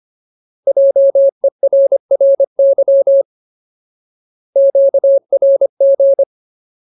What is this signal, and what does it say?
Morse code, heard as a keyed tone, about 25 wpm, 550 Hz. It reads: JERRY QRG
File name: Ultimate Word List / 25WPM Ultimate